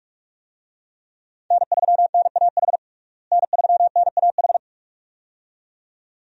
Morse code, heard as a keyed tone, about 43 wpm, 700 Hz. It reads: N3NAH N3NAH
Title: USA Call Signs - 5 Characters - Repeated 2x - 45wpm